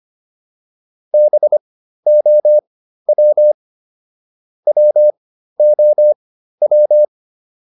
Morse code, heard as a keyed tone, about 25 wpm, spaced as for 13 wpm, 600 Hz. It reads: BOW WOW